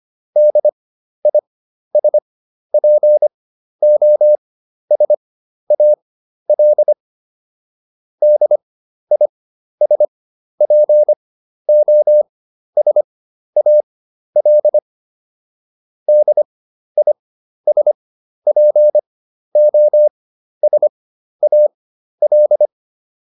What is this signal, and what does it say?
Morse code, heard as a keyed tone, about 25 wpm, spaced as for 12 wpm, 600 Hz. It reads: DISPOSAL DISPOSAL DISPOSAL